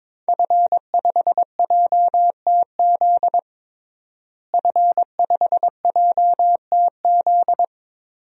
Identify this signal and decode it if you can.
Morse code, heard as a keyed tone, about 22 wpm, 700 Hz. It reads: F5JTZ F5JTZ
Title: DXCC-100 Call Signs - 5 Characters - Repeated 2x - 22wpm